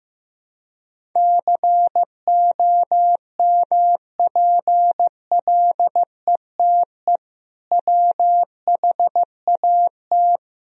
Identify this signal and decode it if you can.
Morse code, heard as a keyed tone, about 15 wpm, 700 Hz. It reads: COMPLETE WHAT
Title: Sentences from Top 1000 Words - Review Part 2 - 15wpm